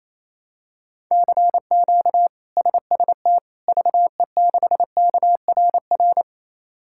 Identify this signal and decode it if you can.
Morse code, heard as a keyed tone, about 28 wpm, 700 Hz. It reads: CQ SST VE6KRR
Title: K1USN SST QSO Practice - 28wpm